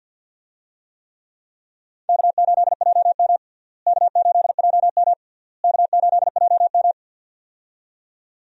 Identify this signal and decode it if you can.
Morse code, heard as a keyed tone, about 47 wpm, 700 Hz. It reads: K8JM K8JM K8JM
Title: USA Call Signs - 4 Characters - Repeated 3x - 50wpm